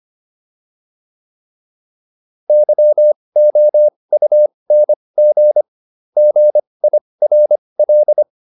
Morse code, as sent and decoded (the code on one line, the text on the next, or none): -.-- --- ..- -. --. / --. .. .-. .-..
YOUNG GIRL